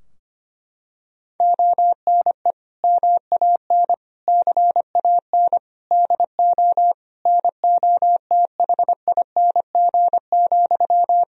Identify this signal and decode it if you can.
Morse code, heard as a keyed tone, about 25 wpm, 700 Hz. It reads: ONE MAN CAN DO NOTHING,
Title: Sentences from Top 1000 Words - Review Part 4 - 25wpm